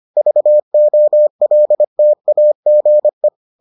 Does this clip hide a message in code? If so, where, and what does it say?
Yes: Morse code from 0.2 to 3.3 s, heard as a keyed tone, about 25 wpm, 600 Hz. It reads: VOLTAGE